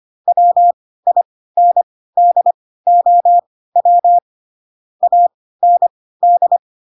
Morse code, heard as a keyed tone, about 25 wpm, spaced as for 16 wpm, 700 Hz. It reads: WINDOW AND
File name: Pinocchio / Pinocchio ch6 10 25 ik1hge